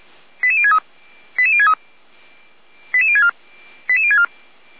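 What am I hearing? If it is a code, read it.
Five-tone selective call (ZVEI), heard as quick runs of five tones, tones of about 70 ms. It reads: 88063 88063 88064 88064
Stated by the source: ZVEI five-tone selective call received with an SDR.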